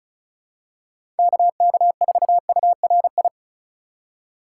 Morse code, heard as a keyed tone, about 36 wpm, 700 Hz. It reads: KK4URI
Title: USA Call Signs - 6 Characters - 2x Character Spacing - 35wpm